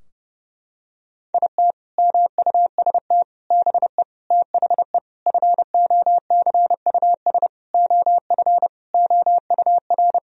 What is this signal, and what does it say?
Morse code, heard as a keyed tone, about 30 wpm, 700 Hz. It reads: IT MUST BE THE FOCUS OF OUR
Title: Sentences from Top 1000 Words - Review Part 4 - 30wpm